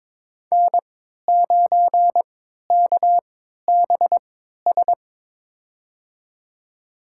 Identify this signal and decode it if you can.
Morse code, heard as a keyed tone, about 22 wpm, spaced as for 12 wpm, 700 Hz. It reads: N9KBS
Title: USA Call Signs - 5 Characters - 3x Character Spacing - 22wpm